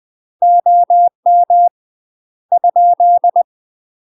Morse code, heard as a keyed tone, about 20 wpm, 700 Hz. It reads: OM ?
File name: UK MASTER QSOs TRAINING / 20WPM / UK MASTER QSOs 0045